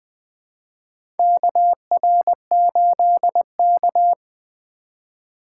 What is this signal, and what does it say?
Morse code, heard as a keyed tone, about 20 wpm, 700 Hz. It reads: KR8K
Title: USA Call Signs - 4 Characters - 3x Character Spacing - 20wpm